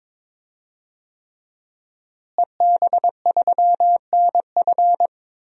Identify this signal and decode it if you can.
Morse code, heard as a keyed tone, about 23 wpm, 700 Hz. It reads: EB3NF
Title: DXCC-100 Call Signs - 5 Characters - Repeated 3x - 22wpm